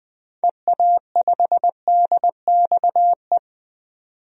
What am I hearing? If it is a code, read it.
Morse code, heard as a keyed tone, about 20 wpm, 700 Hz. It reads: EA5DXE
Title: DXCC-100 Call Signs - 6 Characters - 20wpm